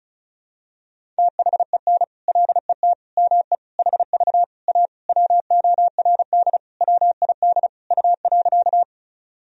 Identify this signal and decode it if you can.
Morse code, heard as a keyed tone, about 36 wpm, 700 Hz. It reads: THEN LET ME HV A WORD WID U.
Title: Sentences from Top 300 Words and Abbreviations 35wpm